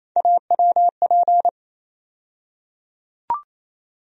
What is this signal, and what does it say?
Morse code, heard as a keyed tone, about 28 wpm, 700 Hz. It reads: AWP
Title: USA Call Sign Postfixes - 3 Letters 28wpm